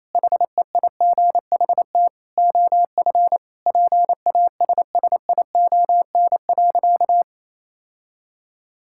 Morse code, heard as a keyed tone, about 28 wpm, 700 Hz. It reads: HEIGHT OF PASSION.
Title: Sentences from Top 1600 Words 28wpm